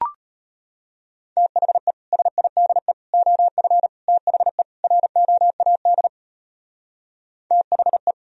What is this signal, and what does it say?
Morse code, heard as a keyed tone, about 32 wpm, 700 Hz. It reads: THE SIDE OF THE ROAD THE